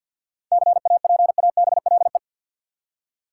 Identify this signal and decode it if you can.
Morse code, heard as a keyed tone, about 51 wpm, 700 Hz. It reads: CAPABLE
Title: Seven Letter Words - 50wpm